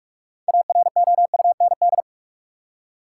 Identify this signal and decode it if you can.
Morse code, heard as a keyed tone, about 46 wpm, 700 Hz. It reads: AROUND
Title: Six Letter Words - 45wpm